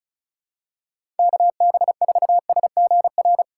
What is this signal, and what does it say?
Morse code, heard as a keyed tone, about 36 wpm, 700 Hz. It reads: KB4SGR
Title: USA Call Signs - 6 Characters - 3x Character Spacing - 35wpm